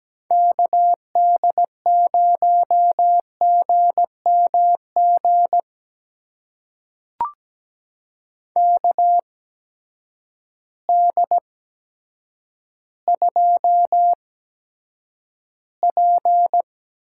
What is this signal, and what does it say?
Morse code, heard as a keyed tone, about 17 wpm, spaced as for 8 wpm, 700 Hz. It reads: KD0GMG KD2P
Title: USA Call Signs - 6 Characters - 8x Character Spacing - 17wpm